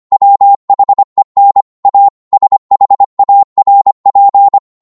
Morse code, heard as a keyed tone, about 25 wpm, 800 Hz. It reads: WHEN A SHARP
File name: Morse / psg-25wpm